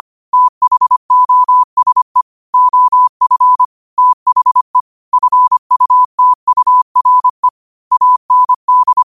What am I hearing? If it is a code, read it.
Morse code, heard as a keyed tone, about 25 wpm, 1000 Hz. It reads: THOSE OF THE FUTURE AND
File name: qotdfiles / 25wpm